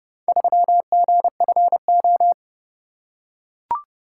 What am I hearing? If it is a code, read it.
Morse code, heard as a keyed tone, about 27 wpm, 700 Hz. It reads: 3GFO